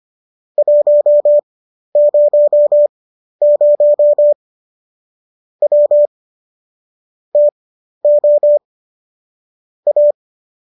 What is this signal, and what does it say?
Morse code, heard as a keyed tone, about 25 wpm, spaced as for 12 wpm, 590 Hz. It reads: 100 W TO A